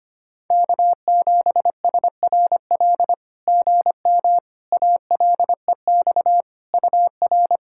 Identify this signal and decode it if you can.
Morse code, heard as a keyed tone, about 25 wpm, 700 Hz. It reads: K7SRL GM ALEX UR
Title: SOTA QSO Practice 25wpm